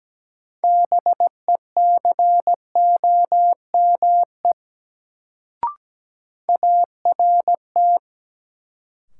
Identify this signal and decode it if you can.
Morse code, heard as a keyed tone, about 17 wpm, 700 Hz. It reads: BECOME ART
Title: Sentences from Top 600 Words - New Word Drill - 17wpm